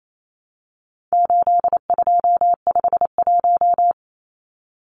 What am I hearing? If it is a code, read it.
Morse code, heard as a keyed tone, about 28 wpm, 700 Hz. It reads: 8251
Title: Numbers - 4-Digit 28wpm